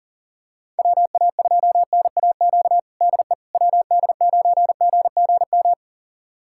Morse code, heard as a keyed tone, about 40 wpm, 700 Hz. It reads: WA2NAQ DE WD9GGM
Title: USA Callsign DE Callsign - 6 Characters - 40wpm